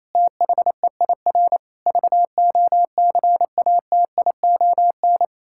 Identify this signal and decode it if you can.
Morse code, heard as a keyed tone, about 27 wpm, 700 Hz. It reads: THEIR VOCATION
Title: Pangram Challenge v2 - 28wpm